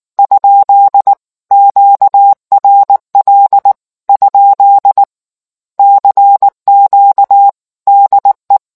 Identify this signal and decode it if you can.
Morse code, heard as a keyed tone, about 19 wpm, 800 Hz. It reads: ? QRL ? CQ DE